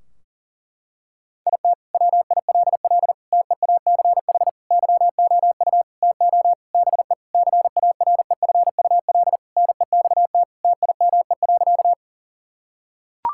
Morse code, heard as a keyed tone, about 40 wpm, 700 Hz. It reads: IT WILL TEACH YOU TO BE CAREFUL NEXT TIME.
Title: Sentences from Top 1600 Words 40wpm